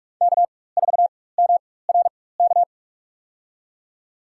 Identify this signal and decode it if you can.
Morse code, heard as a keyed tone, about 47 wpm, spaced as for 21 wpm, 700 Hz. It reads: K4MRK
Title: USA Call Signs - 5 Characters - 4x Character Spacing - 45wpm